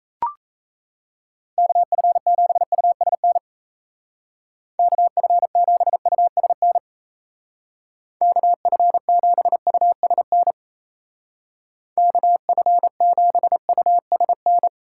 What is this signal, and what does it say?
Morse code, heard as a keyed tone, about 35 wpm, 700 Hz. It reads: KF7USN KF7USN KF7USN KF7USN
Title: Speed Racing - USA 2x3 Call Signs 28wpm